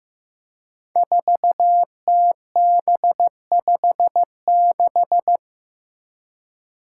Morse code, heard as a keyed tone, about 15 wpm, 700 Hz. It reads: 4TB56
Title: Five Challenging Letter-Number Combinations 15wpm